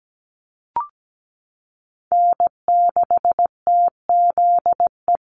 Morse code, heard as a keyed tone, about 17 wpm, 700 Hz. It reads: N6TZE
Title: USA Call Signs Repeated 2x 17wpm